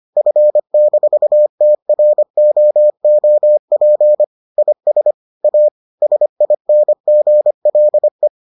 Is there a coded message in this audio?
Yes: Morse code from 0.2 to 8.3 s, heard as a keyed tone, about 25 wpm, 600 Hz. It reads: F-TROOP IS A SINGLE